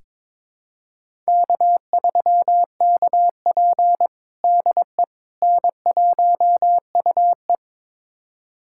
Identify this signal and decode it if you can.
Morse code, heard as a keyed tone, about 22 wpm, 700 Hz. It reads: K3KP DE N1UE
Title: USA Callsign DE Callsign - 4 Characters - 22wpm